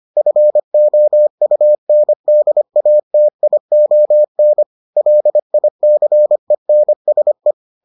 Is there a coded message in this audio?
Yes: Morse code from 0.2 to 7.5 s, heard as a keyed tone, about 25 wpm, 600 Hz. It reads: FOUNDATION LICENSE